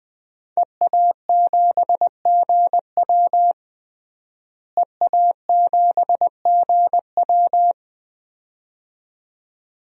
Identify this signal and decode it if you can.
Morse code, heard as a keyed tone, about 20 wpm, 700 Hz. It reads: EA7GW EA7GW
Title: DXCC-100 Call Signs - 5 Characters - Repeated 2x - 20wpm